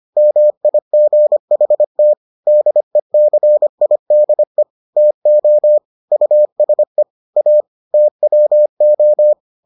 Morse code, heard as a keyed tone, about 25 wpm, 600 Hz. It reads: MIGHT DECIDE TO USE A TWO